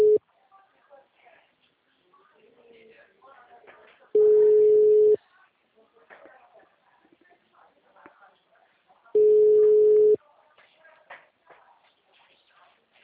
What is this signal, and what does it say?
Ringback tone (Europe, 425 Hz), with speech.